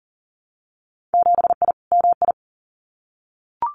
Morse code, heard as a keyed tone, about 38 wpm, 700 Hz. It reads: 7I MI
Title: Winter Field Day Exchanges Encore 40wpm